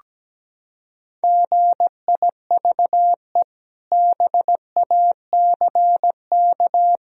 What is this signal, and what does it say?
Morse code, heard as a keyed tone, about 17 wpm, 700 Hz. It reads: GIVE BACK
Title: Sentences from Top 1000 Words - Review Part 1 - 17wpm